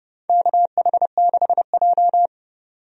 Morse code, heard as a keyed tone, about 30 wpm, 700 Hz. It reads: KH6J
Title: DXCC-100 Call Signs - 4 Characters - 30wpm